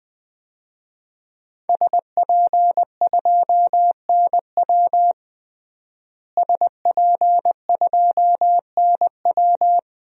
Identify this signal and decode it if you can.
Morse code, heard as a keyed tone, about 21 wpm, 700 Hz. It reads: SP2NW SP2NW
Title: DXCC-100 Call Signs - 5 Characters - Repeated 2x - 20wpm